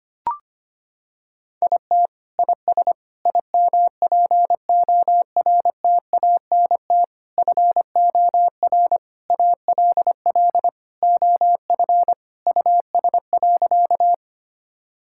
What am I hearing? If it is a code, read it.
Morse code, heard as a keyed tone, about 25 wpm, 700 Hz. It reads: IT IS IMPORTANT FOR ALL OF US.
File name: practice / 25wpm / Sentences with 4-Word Phrases - 25wpm